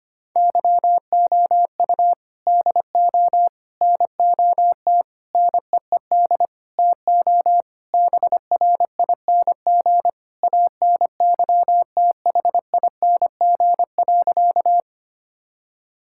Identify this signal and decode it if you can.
Morse code, heard as a keyed tone, about 25 wpm, 700 Hz. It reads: YOU DO NOT NEED TO BRING ANYTHING.
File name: practice / 25wpm / Sentences with 4-Word Phrases - 25wpm